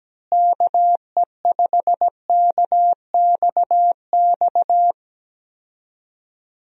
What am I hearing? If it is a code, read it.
Morse code, heard as a keyed tone, about 17 wpm, 700 Hz. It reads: KE5KXX
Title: USA Call Signs - 6 Characters - Repeated 2x - 17wpm